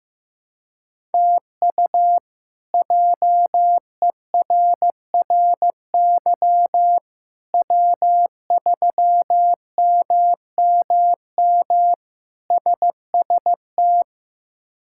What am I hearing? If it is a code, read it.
Morse code, heard as a keyed tone, about 15 wpm, 700 Hz. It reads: TU JERRY W3MMM SST
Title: K1USN SST QSO Practice - 15wpm